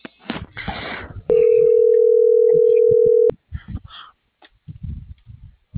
Ringback tone (North America), 440 + 480 Hz.